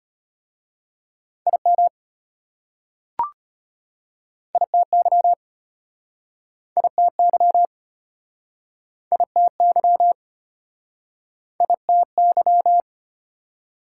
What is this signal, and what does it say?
Morse code, heard as a keyed tone, about 32 wpm, 690 Hz. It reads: IM ITY ITY ITY ITY